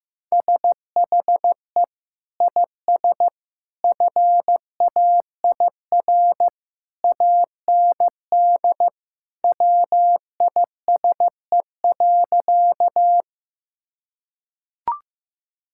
Morse code, heard as a keyed tone, about 16 wpm, 700 Hz. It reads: SHE IS FAIR AND WISE.
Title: Sentences from Top 1600 Words 15wpm